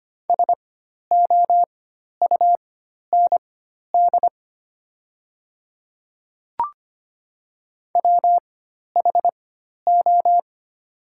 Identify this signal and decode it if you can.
Morse code, heard as a keyed tone, about 25 wpm, spaced as for 12 wpm, 700 Hz. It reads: SOUND WHO
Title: 4x Character Spacing - Top 100 Words - 25wpm